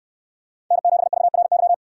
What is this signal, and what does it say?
Morse code, heard as a keyed tone, about 68 wpm, 700 Hz. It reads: N7VRY